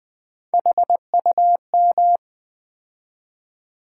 Morse code, heard as a keyed tone, about 20 wpm, 700 Hz. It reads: HUM
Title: Build A Word - 3 Letters At A Time - 8-Letter Words - 20wpm